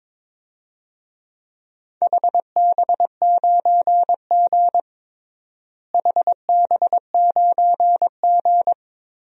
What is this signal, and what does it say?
Morse code, heard as a keyed tone, about 22 wpm, 700 Hz. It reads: HB9G HB9G
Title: DXCC-100 Call Signs - 4 Characters - Repeated 3x - 22wpm